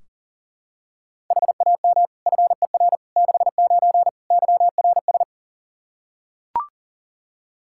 Morse code, heard as a keyed tone, about 40 wpm, 700 Hz. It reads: HAM FER 69 YRS
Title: Practice QSO - Exchange 3 - More Info - 40wpm